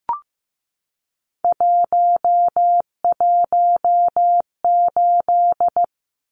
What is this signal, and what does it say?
Morse code, heard as a keyed tone, about 15 wpm, 700 Hz. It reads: 118
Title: Numbers - 3-Digit 15wpm